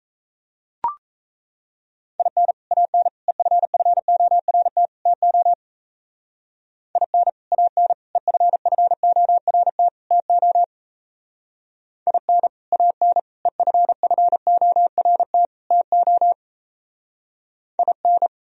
Morse code, heard as a keyed tone, about 33 wpm, 700 Hz. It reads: IN AN EFFORT TO IN AN EFFORT TO IN AN EFFORT TO IN
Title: Speed-Racing - Sets of 4 Words 28wpm